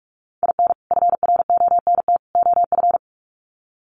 Morse code, heard as a keyed tone, about 44 wpm, 700 Hz. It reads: IN FRONT OF
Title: Sets of 3 Words 45wpm